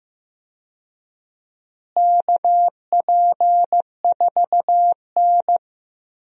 Morse code, heard as a keyed tone, about 15 wpm, 700 Hz. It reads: KP4N
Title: DXCC-100 Call Signs - 4 Characters - Repeated 3x - 15wpm